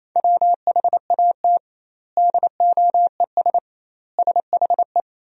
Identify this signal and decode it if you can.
Morse code, heard as a keyed tone, about 28 wpm, 700 Hz. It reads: WHAT DOES SHE
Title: 2x Word-Spacing - Sentences from Top 200 Words - 28wpm